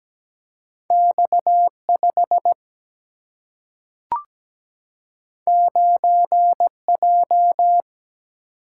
Morse code, heard as a keyed tone, about 17 wpm, 700 Hz. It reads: X5 9J
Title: Single-Letter Single-Number 17wpm